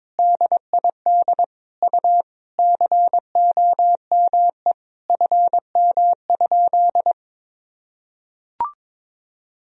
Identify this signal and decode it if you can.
Morse code, heard as a keyed tone, about 22 wpm, 700 Hz. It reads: DID U COME FM?
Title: Sentences from Top 200 Words and Abbreviations 22wpm